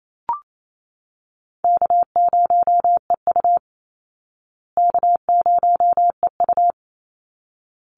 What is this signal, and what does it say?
Morse code, heard as a keyed tone, about 28 wpm, 700 Hz. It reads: K0EU K0EU
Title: Call Signs Repeated 3x - 28wpm